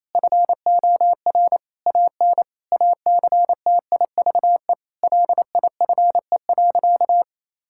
Morse code, heard as a keyed tone, about 28 wpm, 700 Hz. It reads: FOR AN ACTIVE LIFE.
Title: Sentences from Top 1000 Words - Review Part 6 - 28wpm